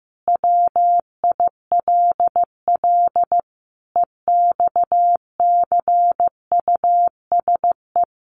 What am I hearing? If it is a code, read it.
Morse code, heard as a keyed tone, about 15 wpm, 700 Hz. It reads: WILL EXCUSE
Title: Sentences from Top 700 Words 15wpm